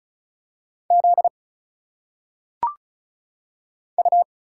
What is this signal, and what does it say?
Morse code, heard as a keyed tone, about 36 wpm, 700 Hz. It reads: Z U
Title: Single Letters - Phonetic Pronunciation 35wpm